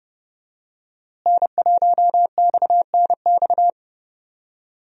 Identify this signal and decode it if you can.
Morse code, heard as a keyed tone, about 32 wpm, 700 Hz. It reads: N1XNX
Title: USA Call Signs - 5 Characters - Repeated 3x - 30wpm